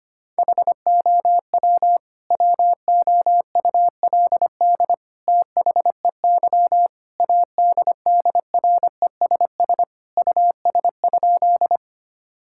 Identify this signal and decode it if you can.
Morse code, heard as a keyed tone, about 25 wpm, 700 Hz. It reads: HOW WOULD THEY ADDRESS US?
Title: Sentences from Top 1000 Words - Review Part 6 - 25wpm